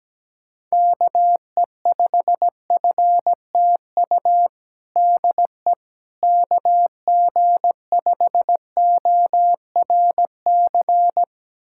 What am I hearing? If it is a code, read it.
Morse code, heard as a keyed tone, about 17 wpm, 700 Hz. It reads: KE5FTU DE KG5ORC
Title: USA Callsign DE Callsign 17wpm